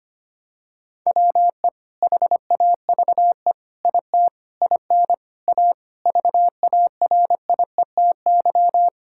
Morse code, heard as a keyed tone, about 25 wpm, 700 Hz. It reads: WE HAVE IT IN A VARIETY
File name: practice / 25wpm / Sentences with 4-Word Phrases - 25wpm